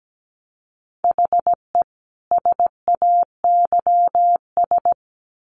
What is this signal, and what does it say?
Morse code, heard as a keyed tone, about 17 wpm, 700 Hz. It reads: HE SAYS
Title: Sets of 2 Words 17wpm